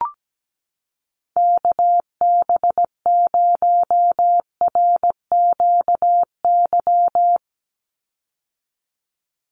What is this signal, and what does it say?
Morse code, heard as a keyed tone, about 17 wpm, 700 Hz. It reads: KB0RQY
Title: USA Call Signs 17wpm